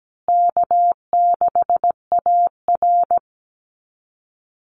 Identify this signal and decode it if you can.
Morse code, heard as a keyed tone, about 17 wpm, 700 Hz. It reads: K6AR
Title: Easy Call Signs Repeated 2x - 17wpm